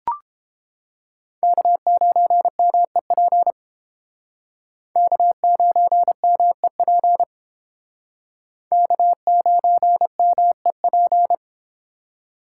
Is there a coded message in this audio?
Yes: Morse code from 1.4 to 11.3 s, heard as a keyed tone, about 30 wpm, 700 Hz. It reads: K9MEP K9MEP K9MEP